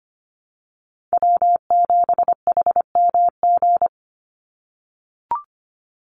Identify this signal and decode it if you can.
Morse code, heard as a keyed tone, about 25 wpm, 700 Hz. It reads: W7HMG
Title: USA Call Signs 25wpm